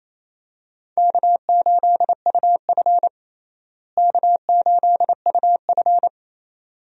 Morse code, heard as a keyed tone, about 28 wpm, 700 Hz. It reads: K8UF K8UF
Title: USA Call Signs - 4 Characters - Repeated 2x - 28wpm